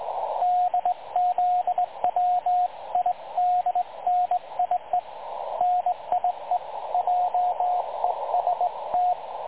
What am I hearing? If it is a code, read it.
Morse code, heard as a keyed tone, about 20 wpm, 720 Hz, with radio noise. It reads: DZWIDNIE NIE JEST
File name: dlazaawansowanych / nagraniazpasma / websdr recording 3565.0kHz